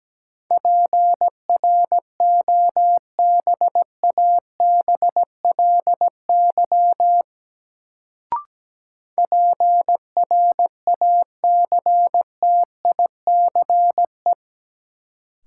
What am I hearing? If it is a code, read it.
Morse code, heard as a keyed tone, about 17 wpm, 700 Hz. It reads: PROBABLY PRACTICE
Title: Sentences from Top 700 Words - New Word Drill - 17wpm